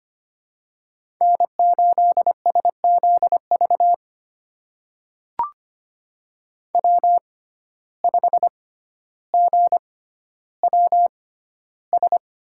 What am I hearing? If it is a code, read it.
Morse code, heard as a keyed tone, about 25 wpm, spaced as for 13 wpm, 700 Hz. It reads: N8SZV W5GWS
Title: USA Call Signs - 5 Characters - 6x Character Spacing - 25wpm